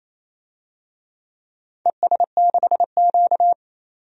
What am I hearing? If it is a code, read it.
Morse code, heard as a keyed tone, about 28 wpm, 700 Hz. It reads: ES6Q
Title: DXCC-100 Call Signs - 4 Characters - 28wpm